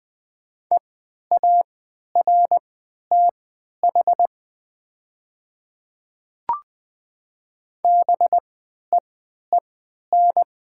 Morse code, heard as a keyed tone, about 20 wpm, spaced as for 11 wpm, 690 Hz. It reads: EARTH BEEN
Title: 3x Character Spacing - Top 300 Words - 20wpm